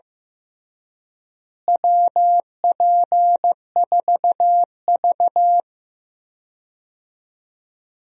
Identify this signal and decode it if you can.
Morse code, heard as a keyed tone, about 15 wpm, 700 Hz. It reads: WP4V
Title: USA Call Signs - 4 Characters - Repeated 3x - 15wpm